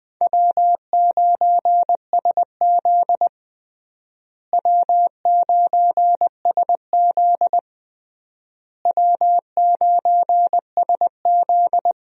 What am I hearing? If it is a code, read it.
Morse code, heard as a keyed tone, about 20 wpm, 700 Hz. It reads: W9SZ W9SZ W9SZ
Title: USA Call Signs - 4 Characters - Repeated 3x - 20wpm